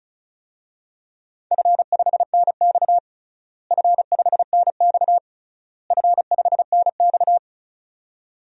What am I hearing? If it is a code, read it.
Morse code, heard as a keyed tone, about 34 wpm, 700 Hz. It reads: F5NX F5NX F5NX
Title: DXCC-100 Call Signs - 4 Characters - Repeated 3x - 35wpm